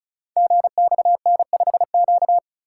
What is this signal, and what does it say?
Morse code, heard as a keyed tone, about 36 wpm, 700 Hz. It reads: GXN5Q
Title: Code Groups - 5 Letter:Numbers - 35wpm